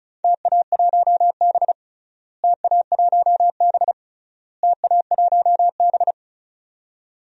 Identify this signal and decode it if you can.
Morse code, heard as a keyed tone, about 34 wpm, 700 Hz. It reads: TA1B TA1B TA1B
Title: DXCC-100 Call Signs - 4 Characters - Repeated 3x - 35wpm